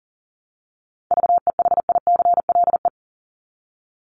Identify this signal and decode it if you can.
Morse code, heard as a keyed tone, about 37 wpm, 700 Hz. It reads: VEHICLE
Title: Top 1751-2000 Words 40wpm